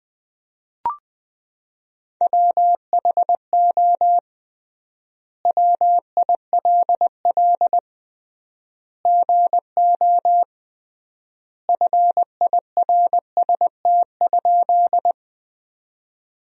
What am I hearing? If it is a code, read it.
Morse code, heard as a keyed tone, about 20 wpm, 700 Hz. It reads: WHO WILL GO FIRST?
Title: 3x Word-Spacing - Sentences from Top 100 Words - 20wpm